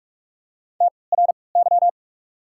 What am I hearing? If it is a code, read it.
Morse code, heard as a keyed tone, about 48 wpm, spaced as for 26 wpm, 690 Hz. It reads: TRY
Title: 3x Character Spacing - Top 200 Words - 45wpm